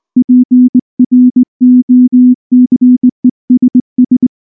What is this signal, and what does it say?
Morse code, heard as a keyed tone, about 20 wpm, 250 Hz. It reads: PROCESS
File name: morse / morse-training-20WPM-10